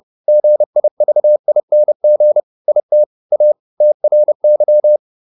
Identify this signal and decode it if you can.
Morse code, heard as a keyed tone, about 30 wpm, 600 Hz. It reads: GIVING IT A TRY